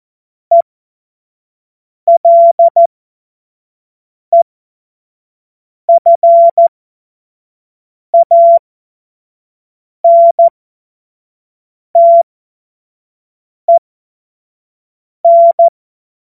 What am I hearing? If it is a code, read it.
Morse code, heard as a keyed tone, about 14 wpm, spaced as for 5 wpm, 680 Hz. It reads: ELEFANTEN
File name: Morsekurs / Witze / Witz 27. September 2022 25 F70